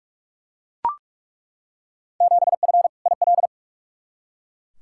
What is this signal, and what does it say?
Morse code, heard as a keyed tone, about 43 wpm, 700 Hz. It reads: 7F IL
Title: Field Day Exchanges Encore - 45wpm